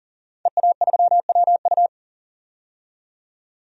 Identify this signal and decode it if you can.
Morse code, heard as a keyed tone, about 40 wpm, 700 Hz. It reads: EA3WU
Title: DXCC-100 Call Signs - 5 Characters - 40wpm